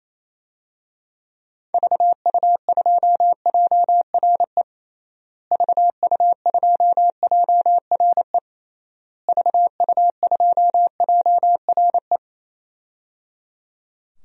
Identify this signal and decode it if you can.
Morse code, heard as a keyed tone, about 28 wpm, 700 Hz. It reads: VU2JRE VU2JRE VU2JRE
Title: DXCC-100 Call Signs - 6 Characters - Repeated 3x - 28wpm